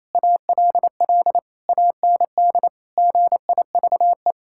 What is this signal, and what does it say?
Morse code, heard as a keyed tone, about 28 wpm, 700 Hz. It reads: ALL AND GIVE
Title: Sentences from Top 1000 Words - Review Part 6 - 28wpm